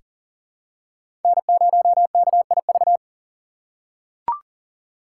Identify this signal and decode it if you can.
Morse code, heard as a keyed tone, about 40 wpm, 700 Hz. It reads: N0KIV
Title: USA Call Signs - 5 Characters - 3x Character Spacing - 40wpm